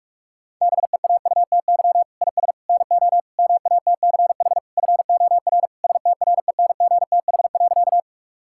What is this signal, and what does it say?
Morse code, heard as a keyed tone, about 46 wpm, 700 Hz. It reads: BEAUTY IS NO MATCH FOR STRENGTH.
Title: Sentences from Top 1600 Words 45wpm